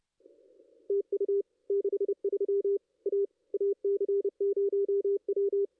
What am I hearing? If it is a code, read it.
Morse code, heard as a keyed tone, about 29 wpm, 440 Hz. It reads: TU 63 A AC0W
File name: contests / ss-cw-k5tr-op-n5rz -2200